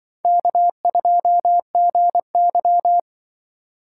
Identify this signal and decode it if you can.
Morse code, heard as a keyed tone, about 24 wpm, 690 Hz. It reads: K2GY
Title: Speed Racing - USA 1x2 Call Signs 20wpm